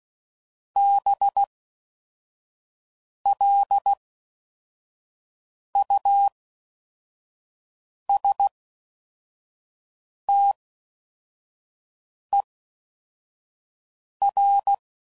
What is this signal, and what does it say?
Morse code, heard as a keyed tone, about 16 wpm, 800 Hz. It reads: B L U S T E R
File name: Suffixes / Special INGs / Fast INGs